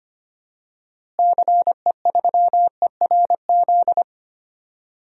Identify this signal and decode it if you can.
Morse code, heard as a keyed tone, about 25 wpm, 700 Hz. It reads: CE3ERZ
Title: DXCC-100 Call Signs - 6 Characters - Repeated 3x - 25wpm